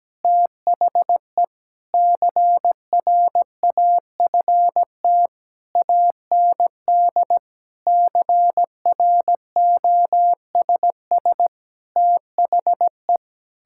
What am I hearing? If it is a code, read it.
Morse code, heard as a keyed tone, about 17 wpm, 700 Hz. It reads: THE CRAFT AND CROSS THE
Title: Sentences from Top 1000 Words - Review Part 6 - 17wpm